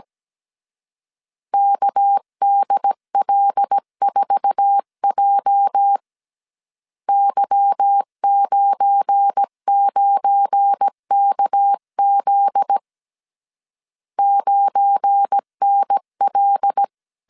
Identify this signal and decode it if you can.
Morse code, heard as a keyed tone, about 17 wpm, 800 Hz. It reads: KDL4J Y99KZ 9NL
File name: 1500C S B 16wpm